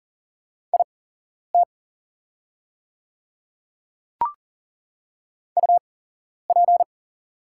Morse code, heard as a keyed tone, about 40 wpm, spaced as for 11 wpm, 690 Hz. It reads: IT UP
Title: 8x Character Spacing - Top 500 Words - 40wpm